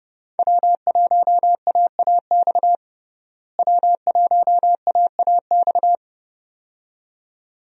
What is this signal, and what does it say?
Morse code, heard as a keyed tone, about 30 wpm, 700 Hz. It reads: W1AAX W1AAX
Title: USA Call Signs - 5 Characters - Repeated 2x - 30wpm